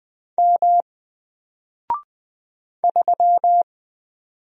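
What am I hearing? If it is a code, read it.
Morse code, heard as a keyed tone, about 20 wpm, 700 Hz. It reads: M 3
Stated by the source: Single letter and number Morse drill at 20 wpm, from N4LCD.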